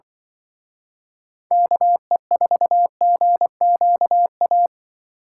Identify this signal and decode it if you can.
Morse code, heard as a keyed tone, about 25 wpm, 700 Hz. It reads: KE4GQA
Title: Speed Racing - USA 2x3 Call Signs 20wpm